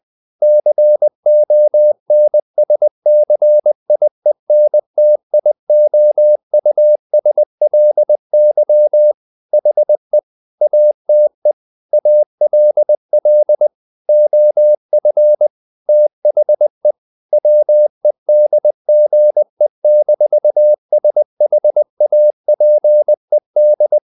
Morse code, heard as a keyed tone, about 20 wpm, 600 Hz. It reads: CONSCIENTIOUSLY HE ATE ALL OF THE WEDGE-SHAPED
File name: the old man and the sea / the old man and the sea part 1 20wpm